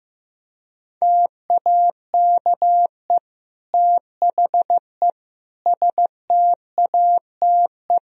Morse code, heard as a keyed tone, about 15 wpm, 700 Hz. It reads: TAKE THE STATE
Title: Sentences from Top 1000 Words - Review Part 3 - 15wpm